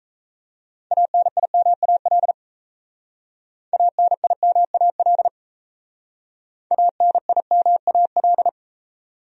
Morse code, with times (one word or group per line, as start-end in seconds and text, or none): 0.9-2.3 ANIMAL
3.7-5.3 ANIMAL
6.7-8.5 ANIMAL